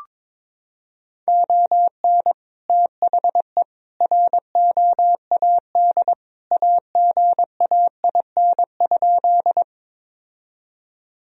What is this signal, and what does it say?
Morse code, heard as a keyed tone, about 22 wpm, 700 Hz. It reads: ON THE ROAD AGAIN?
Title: Sentences from Top 1000 Words - Review Part 2 - 22wpm